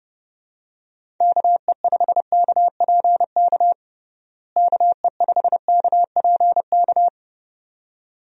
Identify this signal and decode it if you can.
Morse code, heard as a keyed tone, about 30 wpm, 700 Hz. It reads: KE5KPK KE5KPK
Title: USA Call Signs - 6 Characters - Repeated 2x - 30wpm